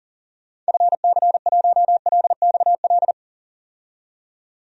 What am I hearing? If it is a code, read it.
Morse code, heard as a keyed tone, about 39 wpm, 700 Hz. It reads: FC1LXL
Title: DXCC-100 Call Signs - 6 Characters - 40wpm